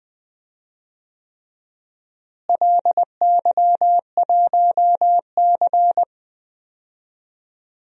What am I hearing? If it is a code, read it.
Morse code, heard as a keyed tone, about 20 wpm, 700 Hz. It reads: LY1C
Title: DXCC-100 Call Signs - 4 Characters - Repeated 3x - 20wpm